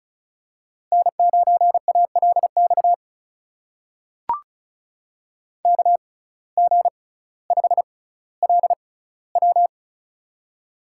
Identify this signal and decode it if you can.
Morse code, heard as a keyed tone, about 34 wpm, spaced as for 18 wpm, 700 Hz. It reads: N9ALX KG5LW
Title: USA Call Signs - 5 Characters - 6x Character Spacing - 35wpm